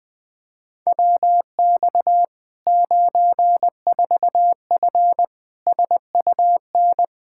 Morse code, heard as a keyed tone, about 20 wpm, 700 Hz. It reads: WX 94F SUN
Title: Practice QSO - Exchange 2 - Info - 20wpm